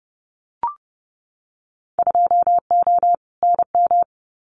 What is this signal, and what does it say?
Morse code, heard as a keyed tone, about 30 wpm, 700 Hz. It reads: 2O NM
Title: Winter Field Day Exchanges 30wpm